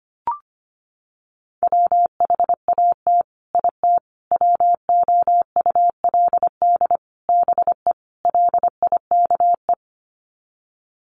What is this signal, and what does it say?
Morse code, heard as a keyed tone, about 25 wpm, 700 Hz. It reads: WHAT IT WOULD BE LIKE
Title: Sets of 5 Words Encore 25wpm